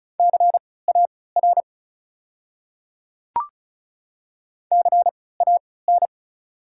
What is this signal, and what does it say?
Morse code, heard as a keyed tone, about 34 wpm, spaced as for 20 wpm, 690 Hz. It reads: CAR CAN
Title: 3x Character Spacing - Top 300 Words - 35wpm